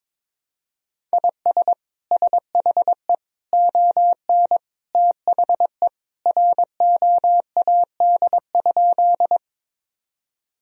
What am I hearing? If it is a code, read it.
Morse code, heard as a keyed tone, about 22 wpm, 700 Hz. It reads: IS SHE ON THE ROAD?
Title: Sentences from Top 1000 Words - Review Part 2 - 22wpm